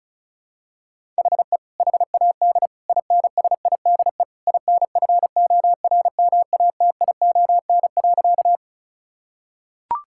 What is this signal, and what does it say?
Morse code, heard as a keyed tone, about 36 wpm, 700 Hz. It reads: HE HAD INSIDE INFORMATION.
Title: Sentences from Top 1000 Words - Review Part 6 - 35wpm